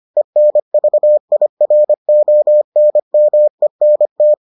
Morse code, heard as a keyed tone, about 25 wpm, 600 Hz. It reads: ENVIRONMENT